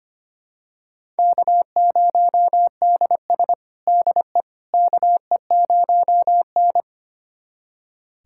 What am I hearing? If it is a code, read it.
Morse code, heard as a keyed tone, about 25 wpm, 700 Hz. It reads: K0DS DE KE0N
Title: USA Callsign DE Callsign - 4 Characters - 25wpm